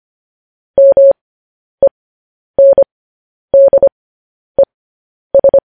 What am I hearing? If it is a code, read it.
Morse code, heard as a keyed tone, about 25 wpm, spaced as for 10 wpm, 550 Hz. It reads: MENDES
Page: News Headlines in Morse Code at 10 WPM